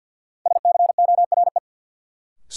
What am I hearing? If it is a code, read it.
Morse code, heard as a keyed tone, about 47 wpm, 700 Hz. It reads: SCORE
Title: Sentences from Top 1600 Words - New Word Drill - 50wpm